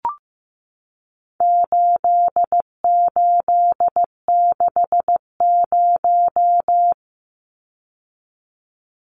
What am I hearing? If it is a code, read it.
Morse code, heard as a keyed tone, about 15 wpm, 700 Hz. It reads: 8860
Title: Numbers - 4-Digit 15wpm